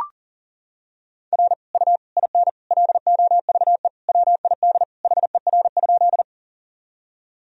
Morse code, heard as a keyed tone, about 37 wpm, 700 Hz. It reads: R U IN LOVE WID HER?
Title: Sentences from Top 400 Words and Abbreviations 40wpm